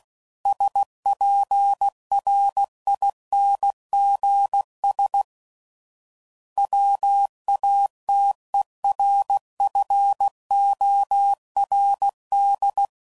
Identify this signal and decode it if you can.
Morse code, heard as a keyed tone, about 16 wpm, 800 Hz. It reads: SPRINGS WATERFORD
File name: CW Code Groups @15WPM / fromSM's / Cities